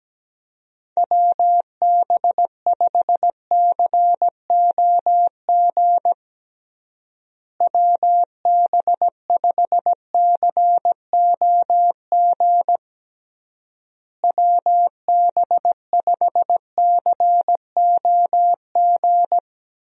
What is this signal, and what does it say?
Morse code, heard as a keyed tone, about 17 wpm, 700 Hz. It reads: WB5COG WB5COG WB5COG
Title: USA Call Signs - 6 Characters - Repeated 3x - 17wpm